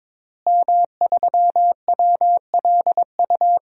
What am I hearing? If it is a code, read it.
Morse code, heard as a keyed tone, about 22 wpm, 700 Hz. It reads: M3WLU
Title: DXCC-100 Call Signs - 5 Characters - Repeated 3x - 22wpm